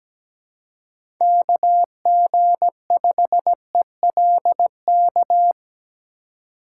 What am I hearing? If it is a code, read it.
Morse code, heard as a keyed tone, about 17 wpm, 700 Hz. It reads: KG5ELK
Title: USA Call Signs - 4x Character Spacing - 17wpm